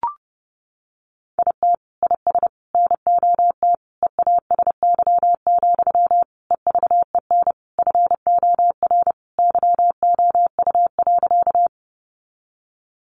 Morse code, heard as a keyed tone, about 30 wpm, 700 Hz. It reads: IT IS NOT EASY, EVEN FOR YOU.
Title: Sentences from Top 700 Words 30wpm